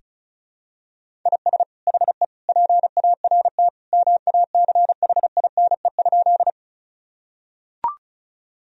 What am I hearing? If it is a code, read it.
Morse code, heard as a keyed tone, about 36 wpm, 700 Hz. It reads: IS HE PART MACHINE?
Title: Sentences from Top 1000 Words - Review Part 2 - 35wpm